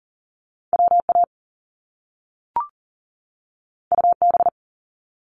Morse code, heard as a keyed tone, about 40 wpm, 700 Hz. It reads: WA UB
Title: Top Two Letter Combinations in Words 40wpm